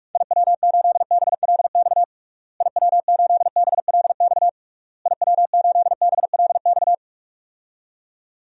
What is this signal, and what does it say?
Morse code, heard as a keyed tone, about 46 wpm, 700 Hz. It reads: IW8BLX IW8BLX IW8BLX
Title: DXCC-100 Call Signs - 6 Characters - Repeated 3x - 45wpm